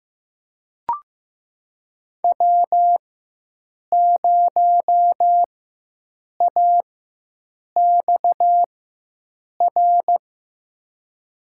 Morse code, heard as a keyed tone, about 15 wpm, spaced as for 7 wpm, 700 Hz. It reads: W0AXR
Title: USA Call Signs - 5 Characters - 4x Character Spacing - 15wpm